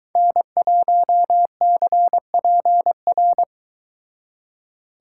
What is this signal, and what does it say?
Morse code, heard as a keyed tone, about 23 wpm, 700 Hz. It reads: N1CPR